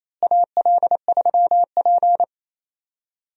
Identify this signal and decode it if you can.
Morse code, heard as a keyed tone, about 28 wpm, 700 Hz. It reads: AL3P
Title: USA Call Signs - 4 Characters - Repeated 3x - 28wpm